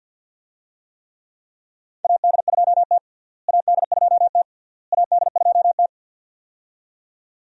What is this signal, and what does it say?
Morse code, heard as a keyed tone, about 49 wpm, 700 Hz. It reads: AD2T AD2T AD2T
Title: USA Call Signs - 4 Characters - Repeated 3x - 50wpm